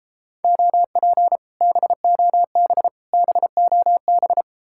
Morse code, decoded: OP BOB BOB

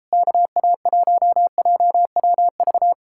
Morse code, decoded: KA1JWV

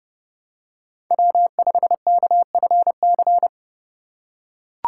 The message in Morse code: W5KFC

